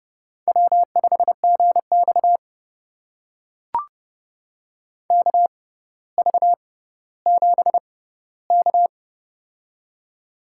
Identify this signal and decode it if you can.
Morse code, heard as a keyed tone, about 30 wpm, spaced as for 15 wpm, 700 Hz. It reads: W5GX KV7K
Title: USA Call Signs - 4 Characters - 6x Character Spacing - 30wpm